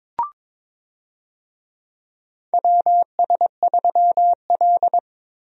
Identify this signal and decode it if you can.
Morse code, heard as a keyed tone, about 22 wpm, 700 Hz. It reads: WS3L
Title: USA Call Signs - 4 Characters - Repeated 2x - 22wpm